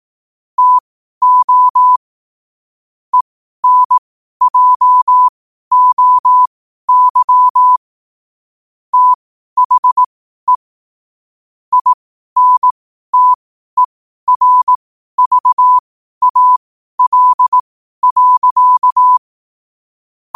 Morse code, heard as a keyed tone, about 18 wpm, spaced as for 12 wpm, 1000 Hz. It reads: TO ENJOY THE INTERVAL.
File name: qotdfiles / 12wpm